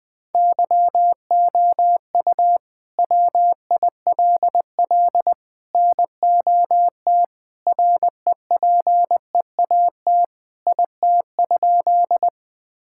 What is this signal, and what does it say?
Morse code, heard as a keyed tone, about 20 wpm, 700 Hz. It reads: YOU WILL NOT REPEAT IT?
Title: Sentences from Top 1600 Words 20wpm